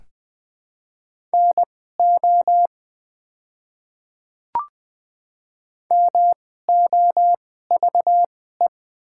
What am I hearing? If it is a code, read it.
Morse code, heard as a keyed tone, about 20 wpm, spaced as for 15 wpm, 690 Hz. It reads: NO MOVE
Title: 2x Character Spacing - Top 200 Words - 20wpm